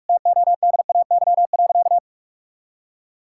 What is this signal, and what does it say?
Morse code, heard as a keyed tone, about 46 wpm, 700 Hz. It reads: TODAY.
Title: Sentences from Top 1000 Words - Review Part 5 - 45wpm